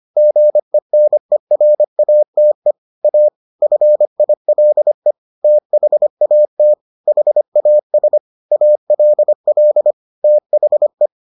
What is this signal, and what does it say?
Morse code, heard as a keyed tone, about 25 wpm, 600 Hz. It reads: GENERATE A FILE THAT HAS ALL THE